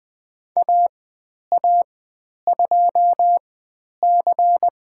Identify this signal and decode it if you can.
Morse code, heard as a keyed tone, about 20 wpm, spaced as for 10 wpm, 700 Hz. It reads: AA2C
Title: 109 Lesson 18g - Letter m - Call Signs 20wpm